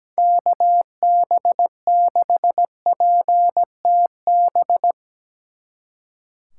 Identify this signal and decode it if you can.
Morse code, heard as a keyed tone, about 17 wpm, 700 Hz. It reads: KB6PTB
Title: Speed Racing - USA 2x3 Call Signs 17wpm